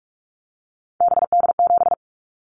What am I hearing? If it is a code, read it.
Morse code, heard as a keyed tone, about 46 wpm, 700 Hz. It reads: BD7
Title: Callsign Prefixes for Top 100 DXCC - 45wpm